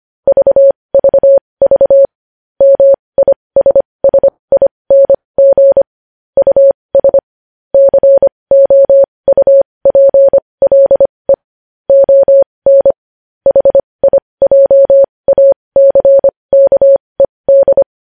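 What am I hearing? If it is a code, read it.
Morse code, heard as a keyed tone, about 25 wpm, spaced as for 20 wpm, 550 Hz. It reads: VVV MISSING US COUPLE ON HIJACKED
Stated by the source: Morse code at 20 wpm.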